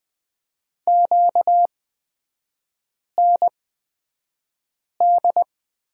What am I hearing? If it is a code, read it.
Morse code, heard as a keyed tone, about 20 wpm, spaced as for 5 wpm, 700 Hz. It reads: QND
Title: 236 Lesson 38e - Letter q - Three Characters 20wpm